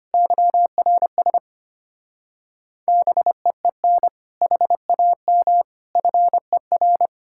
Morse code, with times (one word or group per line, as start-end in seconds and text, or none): 0.1-1.4 YRS
2.9-4.1 BEEN
4.4-5.6 HAM
6.0-7.1 FER